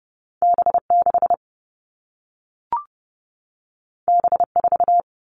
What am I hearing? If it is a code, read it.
Morse code, heard as a keyed tone, about 30 wpm, 700 Hz. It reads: B6 B4
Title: All Cut Numbers - 2-Digit 30wpm